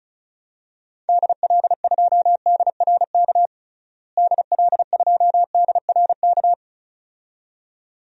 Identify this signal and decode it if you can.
Morse code, heard as a keyed tone, about 36 wpm, 700 Hz. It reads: DL2DRK DL2DRK
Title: DXCC-100 Call Signs - 6 Characters - Repeated 2x - 35wpm